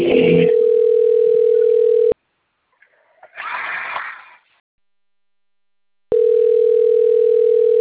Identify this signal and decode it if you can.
Ringback tone (North America), 440 + 480 Hz.